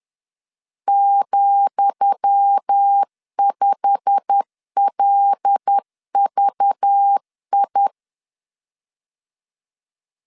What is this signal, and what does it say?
Morse code, heard as a keyed tone, about 10 wpm, 800 Hz. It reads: ,5LVI
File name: LEZIONI / 10 SIGNS / 10D COMA 10wpm 3013